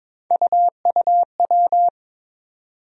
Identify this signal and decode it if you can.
Morse code, heard as a keyed tone, about 22 wpm, 700 Hz. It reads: UUW
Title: USA Call Sign Postfixes - 3 Letters 22wpm